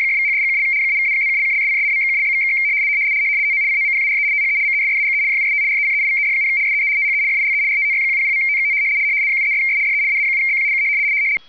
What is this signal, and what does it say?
RTTY, about 100 baud. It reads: !=#$&() -=-+()()':',./()?=// TO COMPARE THE MESSAGE JUST RECEIVED, SEE THE FILE RTTY100.TXT 73'S DE AA7BQ K
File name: rtty170 / rtty100